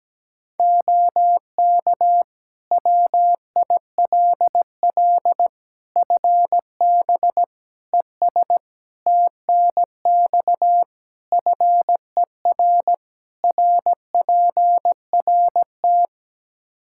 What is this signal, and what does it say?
Morse code, heard as a keyed tone, about 17 wpm, 700 Hz. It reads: OK WILL FB ES TNX FER RPRT
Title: Practice QSO - Exchange 2 - Info - 17wpm